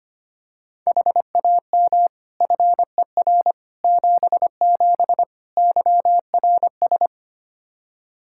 Morse code, heard as a keyed tone, about 25 wpm, 700 Hz. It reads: HAM FER 77 YRS
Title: K1USN SST QSO Practice 25wpm